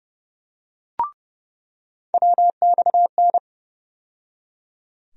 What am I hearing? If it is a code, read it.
Morse code, heard as a keyed tone, about 30 wpm, 700 Hz. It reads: WXN
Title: USA Call Sign Postfixes - 3 Letters 30wpm